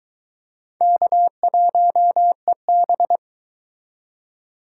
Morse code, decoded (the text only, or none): K1EB